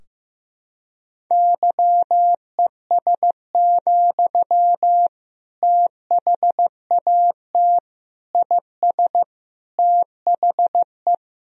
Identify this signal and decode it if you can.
Morse code, heard as a keyed tone, about 16 wpm, 700 Hz. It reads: YES, THAT IS THE
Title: Sentences from Top 1000 Words - Review Part 4 - 15wpm